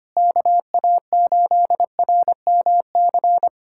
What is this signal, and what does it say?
Morse code, heard as a keyed tone, about 25 wpm, 700 Hz. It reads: KA8RMC
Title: USA Callsign DE Callsign - 6 Characters - 25wpm